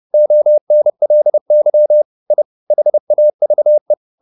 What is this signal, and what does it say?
Morse code, heard as a keyed tone, about 30 wpm, 600 Hz. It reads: ONLY I HAVE
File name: the old man and the sea / the old man and the sea 1 30wpm